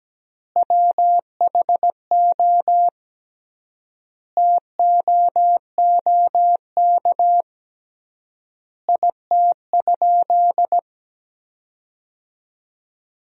Morse code, heard as a keyed tone, about 17 wpm, 700 Hz. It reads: WHO TOOK IT?
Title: 3x Word-Spacing - Sentences from Top 300 Words - 17wpm